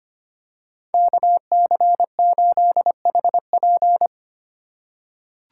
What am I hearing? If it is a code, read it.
Morse code, heard as a keyed tone, about 25 wpm, 700 Hz. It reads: KC8HP
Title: USA Call Signs - 5 Characters - 4x Character Spacing - 25wpm